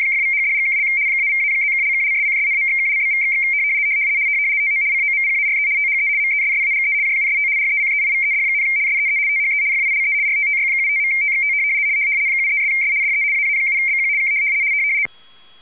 RTTY, about 75 baud. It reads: !=#$&() -=-+()()':',./()?=// TO COMPARE THE MESSAGE JUST RECEIVED, SEE THE FILE RTTY75.TXT 73'S DE AA7BQ K
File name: w ry170 / rtty75.wav / rtty75